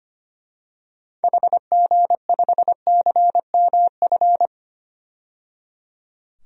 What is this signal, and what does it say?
Morse code, heard as a keyed tone, about 25 wpm, 700 Hz. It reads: HG5CMF
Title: DXCC-100 Call Signs - 6 Characters - 25wpm